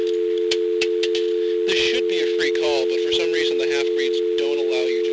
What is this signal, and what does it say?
Dial tone (North America), 350 + 440 Hz, with speech.
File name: blackratchet / recordings / charles-st-payphone